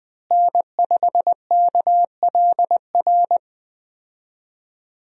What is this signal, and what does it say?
Morse code, heard as a keyed tone, about 20 wpm, 700 Hz. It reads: N5KLR
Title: USA Call Signs - 5 Characters - Repeated 3x - 20wpm